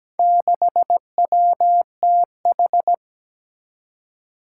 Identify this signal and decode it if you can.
Morse code, heard as a keyed tone, about 17 wpm, 700 Hz. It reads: 6WTH
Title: Four Challenging Letter-Number Combinations 17wpm